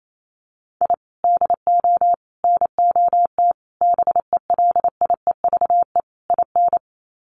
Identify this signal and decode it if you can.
Morse code, heard as a keyed tone, about 28 wpm, 700 Hz. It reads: I DO NOT BELIEVE IN
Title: Sets of 5 Words Encore 28wpm